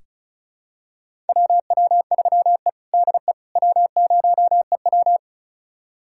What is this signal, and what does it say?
Morse code, heard as a keyed tone, about 36 wpm, 700 Hz. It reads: WW3E DE W0EW
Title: USA Callsign DE Callsign - 4 Characters - 35wpm